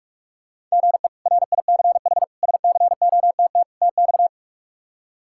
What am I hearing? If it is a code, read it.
Morse code, heard as a keyed tone, about 46 wpm, 700 Hz. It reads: GE RICH SCOTT TX
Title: K1USN SST QSO Practice - 45wpm